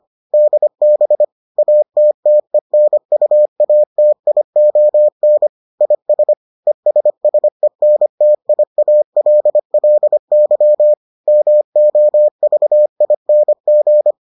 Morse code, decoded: DB ATTENUATION IS ESSENTIALLY MOVING

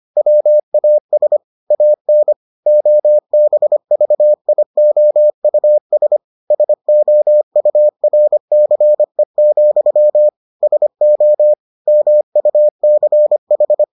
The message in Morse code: WAS AN OBVIOUS SOURCE, SO MUCH

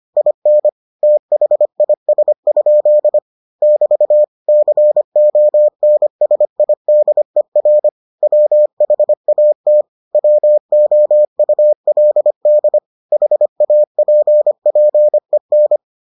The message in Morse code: IN THIS? <BT> CONSIDER WHAT WOULD HAPPEN

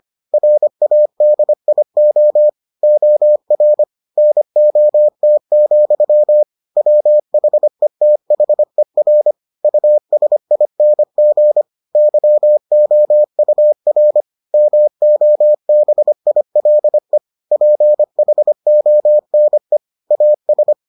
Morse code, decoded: RADIO OR NOT, WHETHER USING YOUR MOBILE PHONE AS

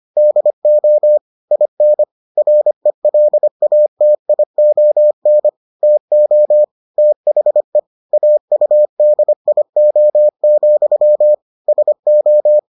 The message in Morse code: DO IN RELATION TO THE AUDIO, SO

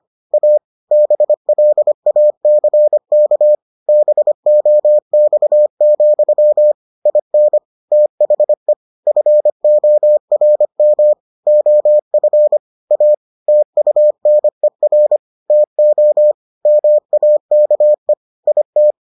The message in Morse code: A BLACK BOX, IN THE FORM OF A TUNER TO MAKE IT

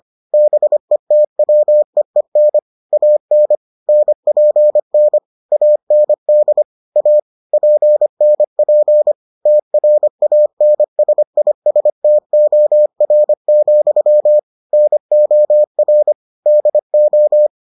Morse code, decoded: BETWEEN AN NPN AND A PNP TRANSISTOR, NOR DO

